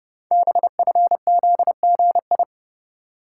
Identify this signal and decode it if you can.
Morse code, heard as a keyed tone, about 30 wpm, 700 Hz. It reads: BFZGI
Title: Code Groups - 5 Letters - 30wpm